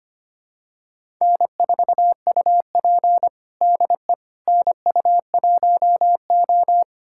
Morse code, heard as a keyed tone, about 25 wpm, 700 Hz. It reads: N4UP DE NU1O
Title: USA Callsign DE Callsign - 4 Characters - 25wpm